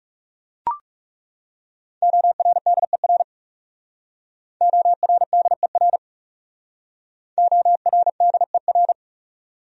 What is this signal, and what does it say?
Morse code, heard as a keyed tone, about 35 wpm, 700 Hz. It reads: ORDER ORDER ORDER
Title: Speed-Racing Top 500 Words 30wpm